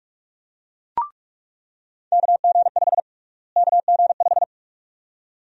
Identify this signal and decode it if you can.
Morse code, heard as a keyed tone, about 42 wpm, 700 Hz. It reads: KG5 KG5
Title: USA Call Sign Prefixes - Repeated 2x - 45wpm